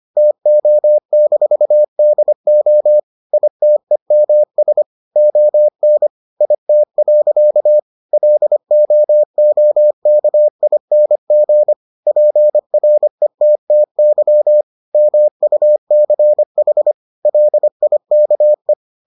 Morse code, heard as a keyed tone, about 25 wpm, 600 Hz. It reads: TO-DO ITEMS ON IT. LOOKING PRETTY MUCH LIKE